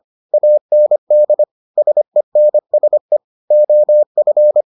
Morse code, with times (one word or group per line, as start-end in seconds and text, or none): 0.3-1.4 AND
1.8-3.2 SENSE
3.5-4.6 OF